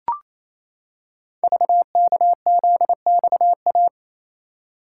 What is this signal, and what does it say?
Morse code, heard as a keyed tone, about 27 wpm, 700 Hz. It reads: VKZXA
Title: Code Groups - 5 Letters - 28wpm